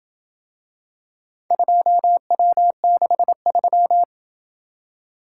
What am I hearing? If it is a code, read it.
Morse code, heard as a keyed tone, about 26 wpm, 700 Hz. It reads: 2W63